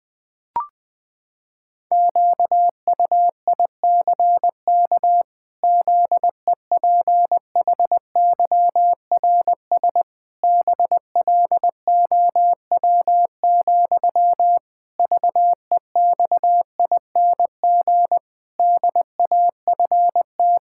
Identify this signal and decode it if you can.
Morse code, heard as a keyed tone, about 19 wpm, 700 Hz. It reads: QUICK ZEPHYRS BLOW, VEXING DAFT
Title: Pangram Challenge v2 - 20wpm